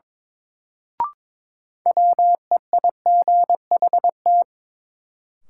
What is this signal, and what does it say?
Morse code, heard as a keyed tone, about 22 wpm, 700 Hz. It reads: WEIGHT
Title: Sentences from Top 800 Words - New Word Drill - 22wpm